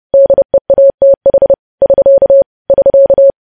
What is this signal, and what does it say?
Morse code, heard as a keyed tone, about 30 wpm, 550 Hz. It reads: DEATH <SK> <SK>
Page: news headlines in morse code at 30 wpm